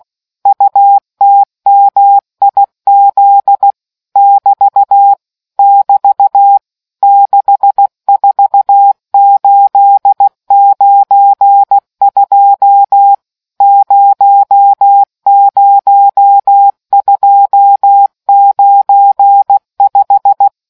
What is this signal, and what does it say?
Morse code, heard as a keyed tone, about 16 wpm, 800 Hz. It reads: UTMIZ <BT> <BT> 64892 00295